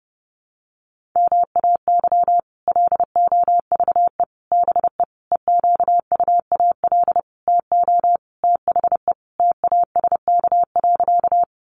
Morse code, heard as a keyed tone, about 30 wpm, 700 Hz. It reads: MAY LOVE BE EQUAL TO THE TASK.
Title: Sentences from Top 1400 Words 30wpm